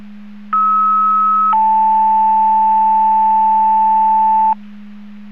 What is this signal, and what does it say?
Two-tone page, heard as a one-second tone, then a three-second tone, about 1290 then 880 Hz. It is heard from 0.5 to 4.5 s.
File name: Portage 154.385 Paging